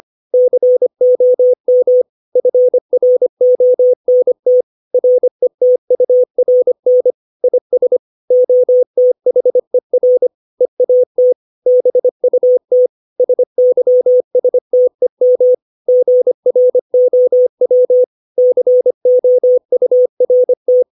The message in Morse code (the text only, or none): COM FRONT RETURN IS OTHER EAT BUT SYSTEM GROW COURT